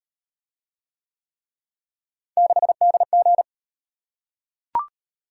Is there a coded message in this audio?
Yes: Morse code from 2.4 to 3.4 s, heard as a keyed tone, about 37 wpm, 700 Hz. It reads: 6DG